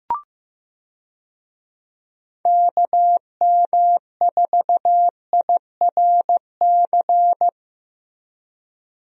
Morse code, heard as a keyed tone, about 15 wpm, 700 Hz. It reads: KM4IRC
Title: USA Call Signs - 6 Characters - Repeated 3x - 15wpm